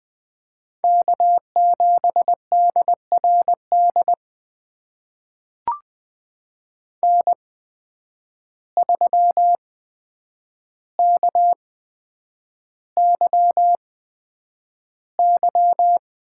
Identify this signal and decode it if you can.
Morse code, heard as a keyed tone, about 20 wpm, spaced as for 9 wpm, 700 Hz. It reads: K7DRD N3KYY
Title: USA Call Signs - 5 Characters - 8x Character Spacing - 20wpm